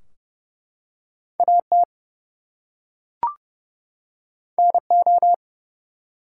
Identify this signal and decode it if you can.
Morse code, heard as a keyed tone, about 29 wpm, 700 Hz. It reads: AT NO
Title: Two Letter Words - 30wpm